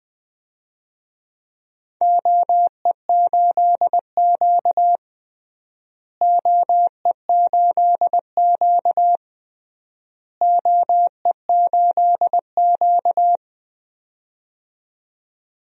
Morse code, heard as a keyed tone, about 20 wpm, 700 Hz. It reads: OE8Q OE8Q OE8Q
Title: DXCC-100 Call Signs - 4 Characters - Repeated 3x - 20wpm